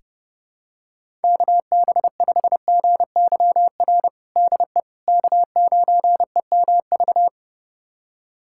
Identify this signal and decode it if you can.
Morse code, heard as a keyed tone, about 30 wpm, 700 Hz. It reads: KB5GYR DE K9EMV
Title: USA Callsign DE Callsign 30wpm